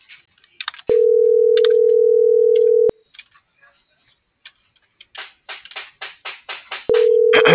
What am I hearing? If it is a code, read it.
Ringback tone (North America), 440 + 480 Hz.
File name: CENTRAL MABE / out xquenda.trinidad - .8456